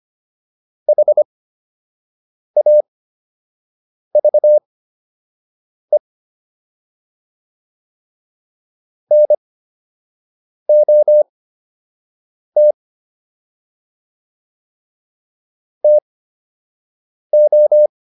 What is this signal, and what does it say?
Morse code, heard as a keyed tone, about 25 wpm, spaced as for 6 wpm, 600 Hz. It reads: HAVE NOT TO